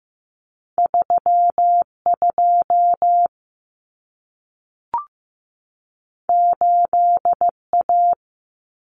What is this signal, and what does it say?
Morse code, heard as a keyed tone, about 15 wpm, 700 Hz. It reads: 32 8A
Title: Common Cut Numbers - 2-Digit 15wpm